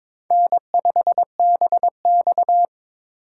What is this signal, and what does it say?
Morse code, heard as a keyed tone, about 22 wpm, 700 Hz. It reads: N5BX
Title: USA Call Signs - 4x Character Spacing - 22wpm